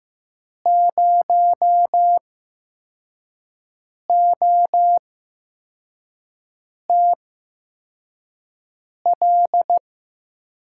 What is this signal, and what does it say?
Morse code, heard as a keyed tone, about 15 wpm, spaced as for 4 wpm, 700 Hz. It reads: 0OTL
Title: USA Call Signs - 8x Character Spacing - 15wpm